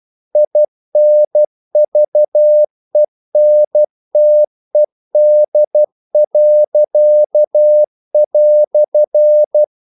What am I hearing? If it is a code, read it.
Morse code, heard as a keyed tone, about 12 wpm, 600 Hz. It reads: INVENTED."
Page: English quotations in CW (telegraphy - morse)